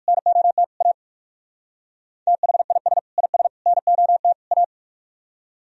Morse code, heard as a keyed tone, about 60 wpm, 700 Hz. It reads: NOT A THIS IS NOT A